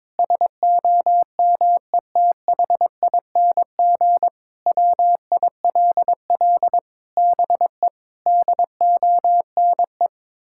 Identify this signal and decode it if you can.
Morse code, heard as a keyed tone, about 22 wpm, 700 Hz. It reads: SOMETHING WILL BE DONE
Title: Sentences from Top 1600 Words 22wpm